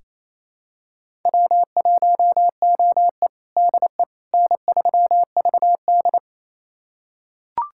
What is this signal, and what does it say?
Morse code, heard as a keyed tone, about 28 wpm, 700 Hz. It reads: W1OE DE N3VD
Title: USA Callsign DE Callsign - 4 Characters - 28wpm